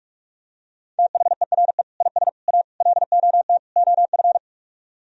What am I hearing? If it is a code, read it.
Morse code, heard as a keyed tone, about 49 wpm, 700 Hz. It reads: THERE IS A LOT OF